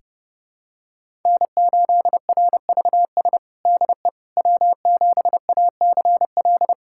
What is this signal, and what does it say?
Morse code, heard as a keyed tone, about 32 wpm, 700 Hz. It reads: N8RVS DE W7ACL
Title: USA Callsign DE Callsign - 5 Characters - 30wpm